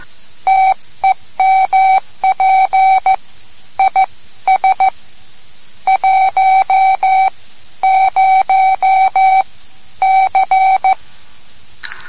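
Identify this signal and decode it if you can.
Morse code, heard as a keyed tone, about 15 wpm, spaced as for 12 wpm, 760 Hz. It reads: TEMP IS 10C
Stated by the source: Morse code lesson, third lesson in a series for learning the Morse alphabet.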